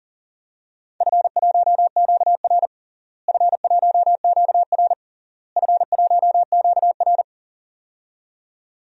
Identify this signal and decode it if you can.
Morse code, heard as a keyed tone, about 40 wpm, 700 Hz. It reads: F1QR F1QR F1QR
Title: DXCC-100 Call Signs - 4 Characters - Repeated 3x - 40wpm